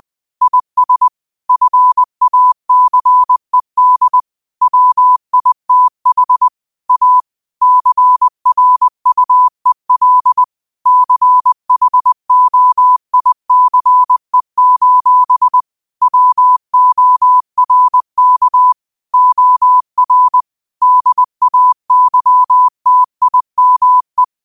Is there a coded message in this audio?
Yes: Morse code from 0.4 to 24.2 s, heard as a keyed tone, about 20 wpm, 1000 Hz. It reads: IS FACED WITH A CRUEL CHOICE: WORK OR DAYTIME